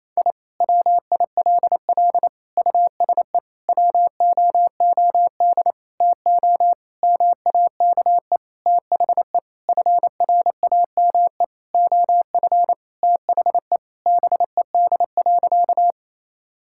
Morse code, heard as a keyed tone, about 28 wpm, 700 Hz. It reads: I WILL USE WOOD TO MAKE THE FRAME OF THE BED.
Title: Sentences from Top 1000 Words - Review Part 6 - 28wpm